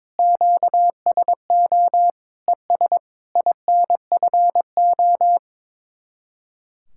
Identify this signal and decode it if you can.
Morse code, heard as a keyed tone, about 22 wpm, 700 Hz. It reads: QSO ES INFO
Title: Speed-Racing Common QSO Phrases Encore - 22wpm